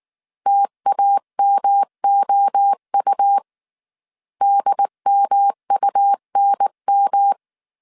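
Morse code, heard as a keyed tone, about 19 wpm, 800 Hz. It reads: TAMOU DMUNM
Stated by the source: Morse code dictation at 18 wpm from the IW3SGT QRS Morse code training course, five-character groups of 300 characters.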